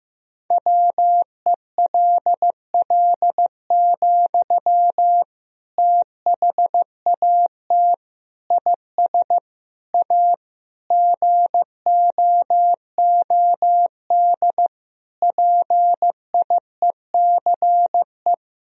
Morse code, heard as a keyed tone, about 15 wpm, 700 Hz. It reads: WELL, THAT IS A GOOD PIECE
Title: Sentences from Top 1000 Words - Review Part 2 - 15wpm